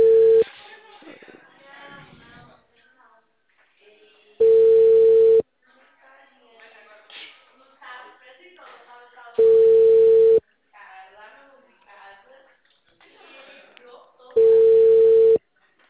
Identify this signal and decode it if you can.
Ringback tone (China, 450 Hz), with speech.
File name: out- -2011- - - .45